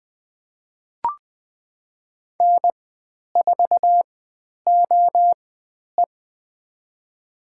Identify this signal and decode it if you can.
Morse code, heard as a keyed tone, about 20 wpm, spaced as for 10 wpm, 700 Hz. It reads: N4OE
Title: 109 Lesson 18g - Letter m - Call Signs 20wpm